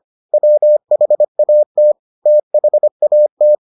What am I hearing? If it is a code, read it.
Morse code, heard as a keyed tone, about 25 wpm, 600 Hz. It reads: WHAT THAT